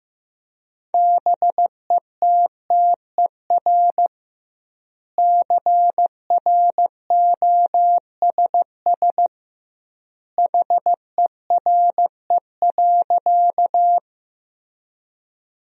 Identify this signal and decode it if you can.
Morse code, heard as a keyed tone, about 15 wpm, 700 Hz. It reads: BETTER CROSS HERE.
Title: 2x Word-Spacing - Sentences from Top 400 Words - 15wpm